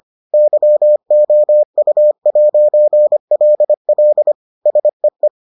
Morse code, heard as a keyed tone, about 25 wpm, 600 Hz. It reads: YOU'LL SEE